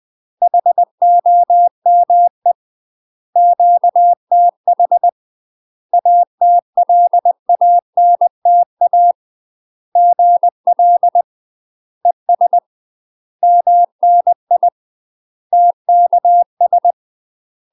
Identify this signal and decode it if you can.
Morse code, heard as a keyed tone, about 20 wpm, 700 Hz. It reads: HOME QTH ATLANTA GL ES MNI TKS
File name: UK MASTER QSOs TRAINING / 20WPM / UK MASTER QSOs 0025